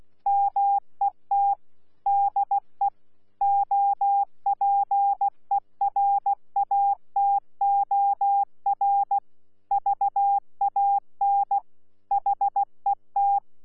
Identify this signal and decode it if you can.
Morse code, heard as a keyed tone, about 16 wpm, 800 Hz. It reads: MET DE OPERATOR VAN HET
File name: Z PI4VRZ / PI4VRZ 14-jun-2025